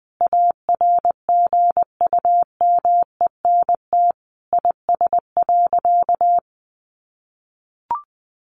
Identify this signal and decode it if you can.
Morse code, heard as a keyed tone, about 20 wpm, 700 Hz. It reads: ARGUMENT IS.
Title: Sentences from Top 1400 Words 20wpm